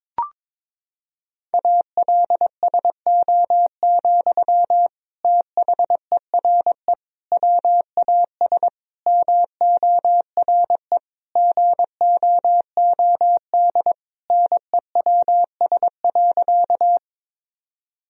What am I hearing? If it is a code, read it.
Morse code, heard as a keyed tone, about 22 wpm, 700 Hz. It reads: ALSO, THERE WAS MORE GOOD NEWS.
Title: Sentences from Top 1000 Words - Review Part 1 - 22wpm